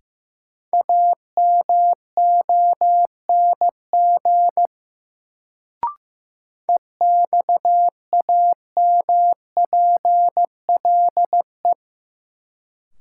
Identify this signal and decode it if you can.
Morse code, heard as a keyed tone, about 15 wpm, 700 Hz. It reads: AMONG EXAMPLE
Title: Sets of 2 Words - New Word Drill - 15wpm